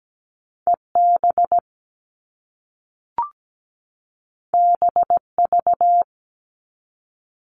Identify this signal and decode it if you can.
Morse code, heard as a keyed tone, about 17 wpm, 700 Hz. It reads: EB BV
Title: All Cut Numbers - 2-Digit 17wpm